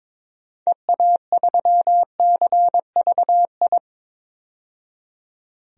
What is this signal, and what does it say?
Morse code, heard as a keyed tone, about 22 wpm, 700 Hz. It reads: EA3CVI
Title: DXCC-100 Call Signs - 6 Characters - 22wpm